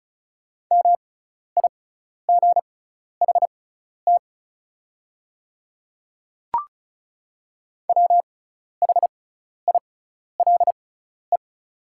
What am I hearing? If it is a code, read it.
Morse code, heard as a keyed tone, about 36 wpm, spaced as for 12 wpm, 690 Hz. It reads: MIGHT WHILE
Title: 6x Character Spacing - Top 300 Words - 35wpm